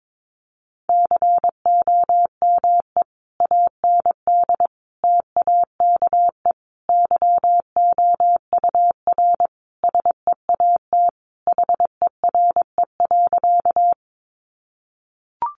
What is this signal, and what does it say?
Morse code, heard as a keyed tone, about 22 wpm, 700 Hz. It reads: COME AND TAKE YOUR SEAT HERE.
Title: Sentences from Top 1500 Words 22wpm